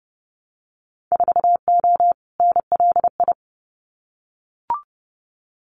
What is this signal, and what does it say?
Morse code, heard as a keyed tone, about 30 wpm, 700 Hz. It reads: 4O NLI
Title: Winter Field Day Exchanges Encore 30wpm